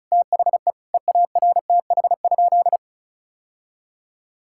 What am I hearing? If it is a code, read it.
Morse code, heard as a keyed tone, about 34 wpm, 700 Hz. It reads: THE EARTH?
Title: Sentences from Top 1600 Words 35wpm